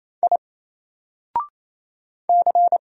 Morse code, heard as a keyed tone, about 28 wpm, 700 Hz. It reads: I C
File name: Single-Letter-Number-28wpm